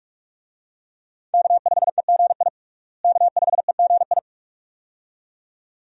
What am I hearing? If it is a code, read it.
Morse code, heard as a keyed tone, about 41 wpm, 700 Hz. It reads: K5EGI K5EGI
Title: USA Call Signs - 5 Characters - Repeated 2x - 45wpm